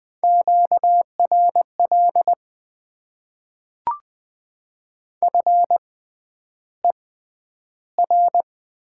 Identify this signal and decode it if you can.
Morse code, heard as a keyed tone, about 20 wpm, spaced as for 10 wpm, 700 Hz. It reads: QRL FER
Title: Top 20 QSO Elements - 6x Character Spacing - 20wpm